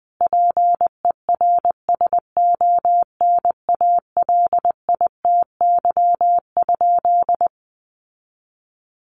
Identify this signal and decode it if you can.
Morse code, heard as a keyed tone, about 20 wpm, 700 Hz. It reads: PERSONALITY?
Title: Sentences from Top 1400 Words 20wpm